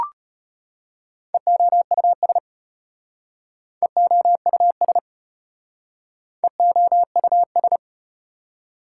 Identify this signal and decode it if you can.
Morse code, heard as a keyed tone, about 35 wpm, 700 Hz. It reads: EOUS EOUS EOUS